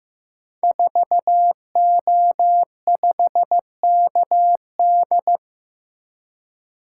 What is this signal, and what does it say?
Morse code, heard as a keyed tone, about 15 wpm, 700 Hz. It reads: 4O5KD
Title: Code Groups - 5 Letter:Numbers - 15wpm